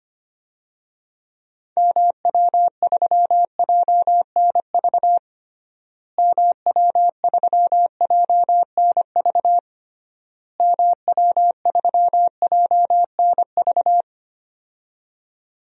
Morse code, heard as a keyed tone, about 25 wpm, 700 Hz. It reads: MW3JNV MW3JNV MW3JNV
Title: DXCC-100 Call Signs - 6 Characters - Repeated 3x - 25wpm